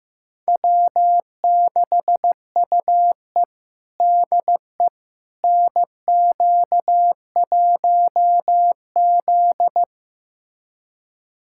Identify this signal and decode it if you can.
Morse code, heard as a keyed tone, about 15 wpm, 700 Hz. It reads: W6UE DE NQ1Z
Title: USA Callsign DE Callsign - 4 Characters - 15wpm